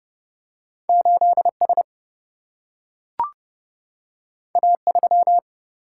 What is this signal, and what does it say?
Morse code, heard as a keyed tone, about 30 wpm, 700 Hz. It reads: 8S A3
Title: Single-Letter Single-Number 30wpm